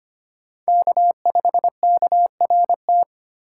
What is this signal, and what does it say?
Morse code, heard as a keyed tone, about 32 wpm, 700 Hz. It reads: K5KRT